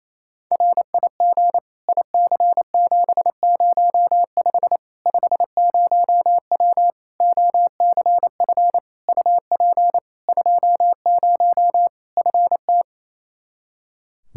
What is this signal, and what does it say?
Morse code, heard as a keyed tone, about 28 wpm, 700 Hz. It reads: RIG IC705 50W OCF UP 20 FT
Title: Practice QSO - Exchange 2 - Info - 28wpm